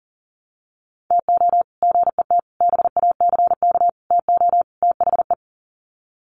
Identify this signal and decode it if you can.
Morse code, heard as a keyed tone, about 40 wpm, 700 Hz. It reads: TO GET BACK TO THE
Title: Sets of 5 Words Encore 40wpm